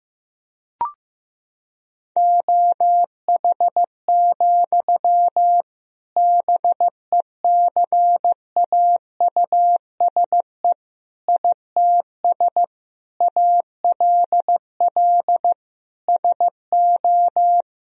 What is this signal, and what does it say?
Morse code, heard as a keyed tone, about 15 wpm, 700 Hz. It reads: OH, BECAUSE ITS ALL SO
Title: Sentences from Top 1000 Words - Review Part 3 - 15wpm